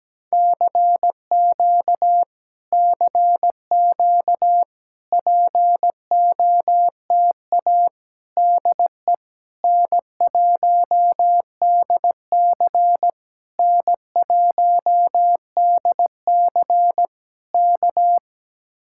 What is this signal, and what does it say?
Morse code, heard as a keyed tone, about 17 wpm, 700 Hz. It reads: CQ CQ POTA DE N1DC N1DC K
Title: Speed-Racing Common QSO Phrases Encore - 17wpm